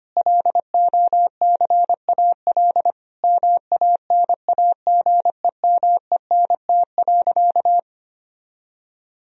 Morse code, heard as a keyed tone, about 25 wpm, 700 Hz. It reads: LOCAL MANAGEMENT.
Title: Sentences from Top 1000 Words - Review Part 4 - 25wpm